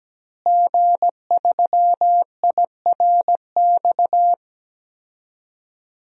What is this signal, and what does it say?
Morse code, heard as a keyed tone, about 17 wpm, 700 Hz. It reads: G3IRX
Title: DXCC-100 Call Signs - 5 Characters - Repeated 3x - 17wpm